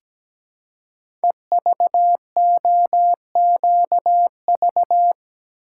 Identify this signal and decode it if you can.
Morse code, heard as a keyed tone, about 17 wpm, 700 Hz. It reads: EVOQV
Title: Code Groups - 5 Letters - 17wpm